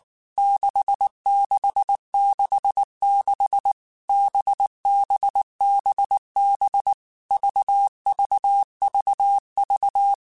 Morse code, heard as a keyed tone, about 19 wpm, 800 Hz. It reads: 6666 BBBB VVVV
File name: Basic Letter Groups By7Groups / VB46 / VB46@18